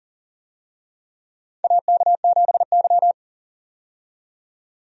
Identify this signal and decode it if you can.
Morse code, heard as a keyed tone, about 40 wpm, 700 Hz. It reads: AK7Y
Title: USA Call Signs - 4 Characters - 40wpm